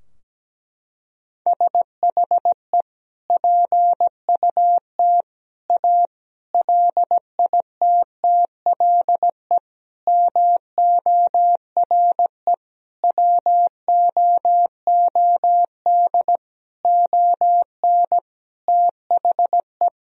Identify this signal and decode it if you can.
Morse code, heard as a keyed tone, about 17 wpm, 700 Hz. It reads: SHE PUT A LITTLE MORE WOOD ON THE
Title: Sentences from Top 1000 Words - Review Part 2 - 17wpm